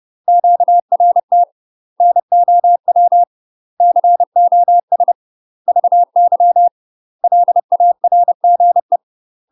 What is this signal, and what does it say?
Morse code, heard as a keyed tone, about 30 wpm, 700 Hz. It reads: QRT NOW COS VY LARGE
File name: UK MASTER QSOs TRAINING / 30WPM / UK MASTER QSOs 0022